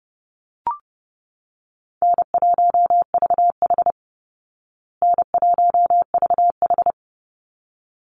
Morse code, heard as a keyed tone, about 30 wpm, 700 Hz. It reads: N1VH N1VH
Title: Easy Call Signs Repeated 2x - 30wpm